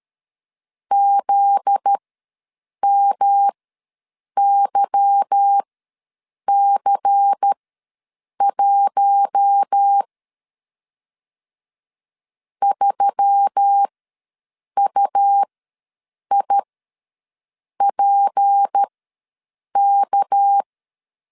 Morse code, heard as a keyed tone, about 13 wpm, spaced as for 7 wpm, 800 Hz. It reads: ZMYC1 3UIPK
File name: LEZIONI / 8 YJ45 / 8 Y ws 12wpm 3409